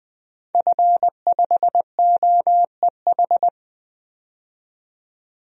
Morse code, heard as a keyed tone, about 20 wpm, 700 Hz. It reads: F5OEH
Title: DXCC-100 Call Signs - 5 Characters - 20wpm